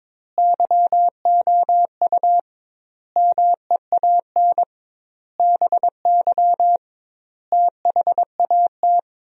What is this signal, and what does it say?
Morse code, heard as a keyed tone, about 22 wpm, 700 Hz. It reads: YOU MEAN BY THAT
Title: 2x Word-Spacing - Sentences from Top 400 Words - 22wpm